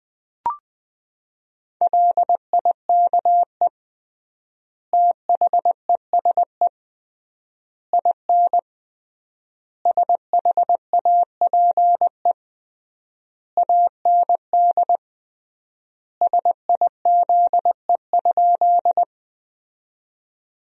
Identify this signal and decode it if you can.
Morse code, heard as a keyed tone, about 21 wpm, 700 Hz. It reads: LIKE THESE IN SHAPE AND SIZE?
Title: 3x Word-Spacing - Sentences from Top 500 Words - 20wpm